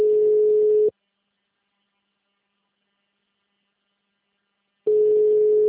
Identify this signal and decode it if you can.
Ringback tone (Europe, 425 Hz).